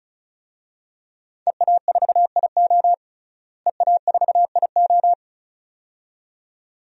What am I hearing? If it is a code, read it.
Morse code, heard as a keyed tone, about 35 wpm, 700 Hz. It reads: EA4IO EA4IO
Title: DXCC-100 Call Signs - 5 Characters - Repeated 2x - 35wpm